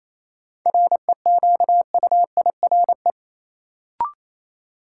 Morse code, heard as a keyed tone, about 28 wpm, 700 Hz. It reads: REQUIRE
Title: Sentences from Top 900 Words - New Word Drill - 28wpm